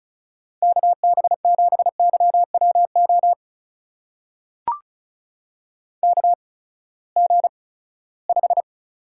Morse code, heard as a keyed tone, about 36 wpm, spaced as for 20 wpm, 700 Hz. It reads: KB7YWO KG5
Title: USA Call Signs - 6 Characters - 8x Character Spacing - 35wpm